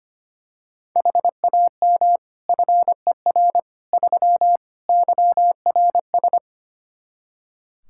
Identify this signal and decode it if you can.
Morse code, heard as a keyed tone, about 25 wpm, 700 Hz. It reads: HAM FER 3 YRS
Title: Practice QSO - Exchange 3 - More Info - 25wpm